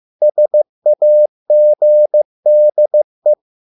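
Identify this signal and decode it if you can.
Morse code, heard as a keyed tone, about 15 wpm, 600 Hz. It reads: SAGDE
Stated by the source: Morse code at a slow 15 wpm from CW med Gnister.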